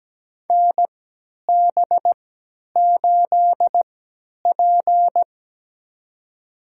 Morse code, heard as a keyed tone, about 17 wpm, spaced as for 10 wpm, 700 Hz. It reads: NB8P
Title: USA Call Signs - 4 Characters - 3x Character Spacing - 17wpm